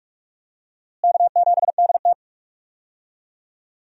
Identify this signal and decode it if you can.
Morse code, heard as a keyed tone, about 46 wpm, 700 Hz. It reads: K7DT
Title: USA Call Signs - 4 Characters - Repeated 2x - 45wpm